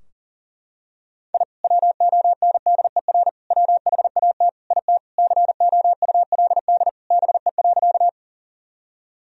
Morse code, heard as a keyed tone, about 40 wpm, 700 Hz. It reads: I WONDER WHAT IT COULD BE.
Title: Sentences from Top 1000 Words - Review Part 3 - 40wpm